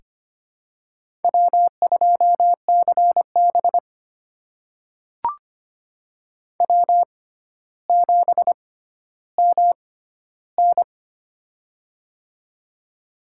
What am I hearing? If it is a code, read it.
Morse code, heard as a keyed tone, about 25 wpm, spaced as for 13 wpm, 700 Hz. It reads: W2CB W7MN
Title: USA Call Signs - 6x Character Spacing - 25wpm